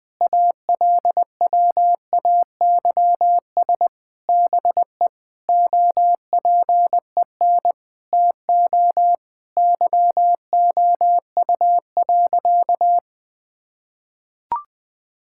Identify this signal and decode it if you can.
Morse code, heard as a keyed tone, about 20 wpm, 700 Hz. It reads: ALWAYS BE OPEN TO YOU.
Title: Sentences from Top 1600 Words 20wpm